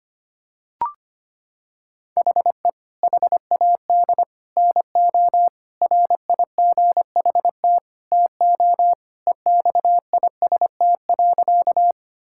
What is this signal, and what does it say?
Morse code, heard as a keyed tone, about 25 wpm, 700 Hz. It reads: HE HAD NO RIGHT TO EXIST.
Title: Sentences from Top 1000 Words - Review Part 5 - 25wpm